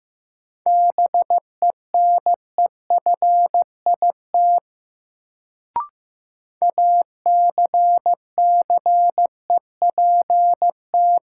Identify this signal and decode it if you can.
Morse code, heard as a keyed tone, about 16 wpm, 700 Hz. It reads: BENEFIT ACCEPT
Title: Sentences from Top 1300 Words - New Word Drill Alt - 15wpm